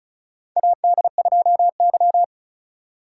Morse code, heard as a keyed tone, about 36 wpm, 700 Hz. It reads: AD2Y
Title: USA Call Signs - 8x Character Spacing - 35wpm